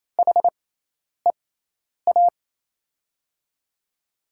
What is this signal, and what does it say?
Morse code, heard as a keyed tone, about 27 wpm, spaced as for 10 wpm, 700 Hz. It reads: HEA
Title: USA Call Signs - 6 Characters - 6x Character Spacing - 28wpm